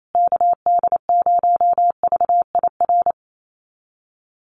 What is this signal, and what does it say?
Morse code, heard as a keyed tone, about 28 wpm, 700 Hz. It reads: KD0VIR
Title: USA Call Signs Repeated 3x 28wpm